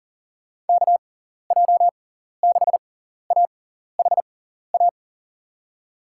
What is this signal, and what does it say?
Morse code, heard as a keyed tone, about 40 wpm, spaced as for 14 wpm, 700 Hz. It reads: KJ6AHA
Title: USA Call Signs - 6 Characters - 6x Character Spacing - 40wpm